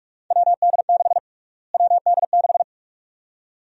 Morse code, heard as a keyed tone, about 42 wpm, 700 Hz. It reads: WD6 WD6
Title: USA Call Sign Prefixes - Repeated 2x - 45wpm